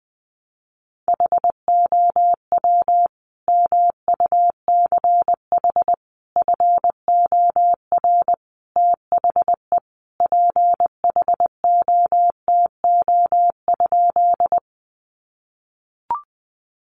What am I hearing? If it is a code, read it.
Morse code, heard as a keyed tone, about 20 wpm, 700 Hz. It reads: HOW MUCH FOR THE PHOTO?
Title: Sentences from Top 1200 Words 20wpm